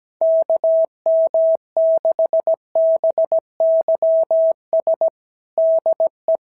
Morse code, heard as a keyed tone, about 16 wpm, 650 Hz. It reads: KM6BYS DE
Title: SOTA QSO Practice 17wpm